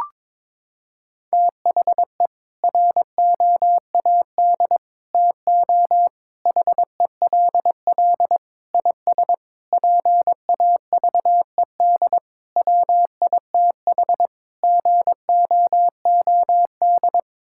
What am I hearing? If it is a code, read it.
Morse code, heard as a keyed tone, about 21 wpm, 700 Hz. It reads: THE ROAD TO HELL IS PAVED WITH GOOD
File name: practice / 22wpm / Sets of 8 Words - 22wpm